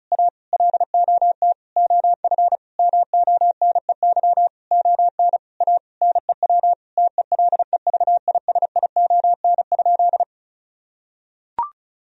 Morse code, heard as a keyed tone, about 36 wpm, 700 Hz. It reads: A LOT OF MONEY ON A NEW TELEVISION?
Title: Sentences from Top 1000 Words - Review Part 4 - 35wpm